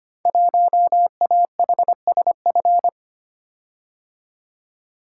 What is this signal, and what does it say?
Morse code, heard as a keyed tone, about 25 wpm, 700 Hz. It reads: 1AHSF
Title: Code Groups - 5 Letter:Numbers - 25wpm